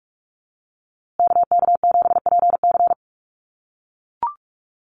Morse code, heard as a keyed tone, about 44 wpm, 700 Hz. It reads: KK7PC
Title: USA Call Signs Repeated 3x 45wpm